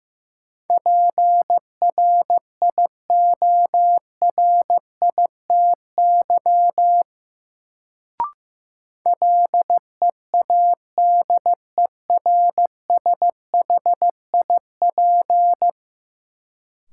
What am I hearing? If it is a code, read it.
Morse code, heard as a keyed tone, about 15 wpm, 700 Hz. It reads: PRIORITY LEADERSHIP
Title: Sentences from Top 1600 Words - New Word Drill Alt - 15wpm